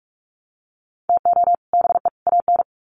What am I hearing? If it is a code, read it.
Morse code, heard as a keyed tone, about 44 wpm, 700 Hz. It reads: TO BE AN
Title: Sets of 3 Words 45wpm